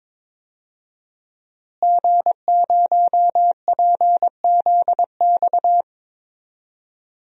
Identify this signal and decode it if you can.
Morse code, heard as a keyed tone, about 22 wpm, 700 Hz. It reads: G0PZX
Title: DXCC-100 Call Signs - 5 Characters - Repeated 3x - 22wpm